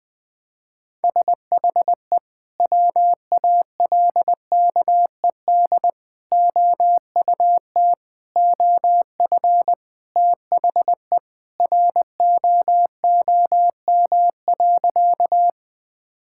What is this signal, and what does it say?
Morse code, heard as a keyed tone, about 21 wpm, 700 Hz. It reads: SHE WALKED OUT OF THE ROOM.
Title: Sentences with 3-Word Phrases - 20wpm